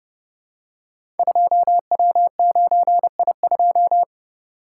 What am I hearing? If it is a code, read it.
Morse code, heard as a keyed tone, about 30 wpm, 700 Hz. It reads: 2W9I2
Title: Code Groups - 5 Letter:Numbers - 30wpm